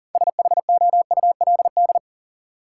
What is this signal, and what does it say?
Morse code, heard as a keyed tone, about 40 wpm, 700 Hz. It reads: SHOULD
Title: Sentences from Top 300 Words - New Word Drill - 40wpm